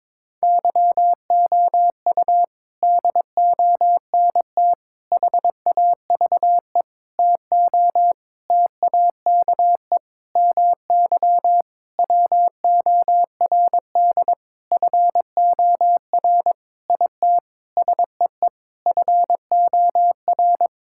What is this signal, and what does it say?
Morse code, heard as a keyed tone, about 22 wpm, 700 Hz. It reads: YOU DONT HAVE TO TAKE MY WORD FOR IT SEE FOR
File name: practice / 22wpm / Sentences with 8-Word Phrases - 22wpm